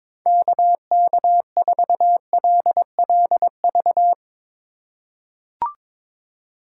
Morse code, heard as a keyed tone, about 22 wpm, 700 Hz. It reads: KK4LLV
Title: USA Call Signs - 6 Characters - 3x Character Spacing - 22wpm